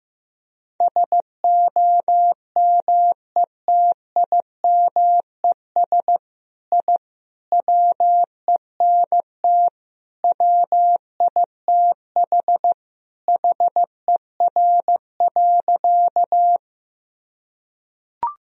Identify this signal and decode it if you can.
Morse code, heard as a keyed tone, about 15 wpm, 700 Hz. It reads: SOMETIMES I WENT WITH HER.
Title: Sentences from Top 1000 Words - Review Part 3 - 15wpm